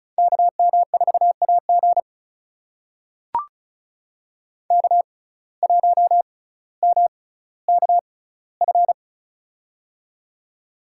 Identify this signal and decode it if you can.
Morse code, heard as a keyed tone, about 34 wpm, spaced as for 18 wpm, 700 Hz. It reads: KM4AG K1MKF
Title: USA Call Signs - 5 Characters - 6x Character Spacing - 35wpm